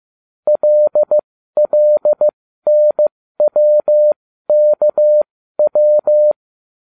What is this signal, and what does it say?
Morse code, heard as a keyed tone, about 15 wpm, spaced as for 12 wpm, 600 Hz. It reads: LLNWKW